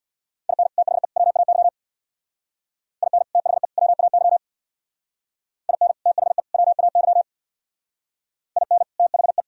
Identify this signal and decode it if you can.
Morse code, heard as a keyed tone, about 58 wpm, 700 Hz. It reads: IN THE WAY IN THE WAY IN THE WAY IN THE
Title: Speed-Racing - Sets of 3 Words 50wpm